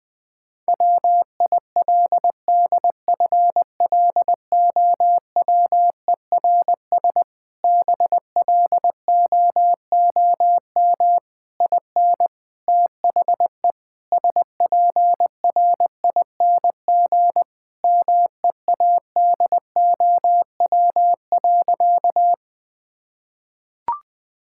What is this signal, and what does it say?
Morse code, heard as a keyed tone, about 20 wpm, 700 Hz. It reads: WILDFLOWERS BLOOM IN THE SPRING MEADOW.
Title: Difficult Sentences with Hints - 20wpm